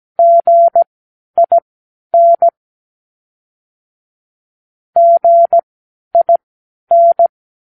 Morse code, heard as a keyed tone, about 17 wpm, spaced as for 10 wpm, 680 Hz. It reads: GIN GIN
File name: k7qo / Practicewords